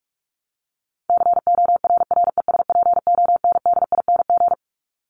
Morse code, heard as a keyed tone, about 44 wpm, 700 Hz. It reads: CORRESPONDING
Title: ING Suffix - Top 100 Words with Base Words 45wpm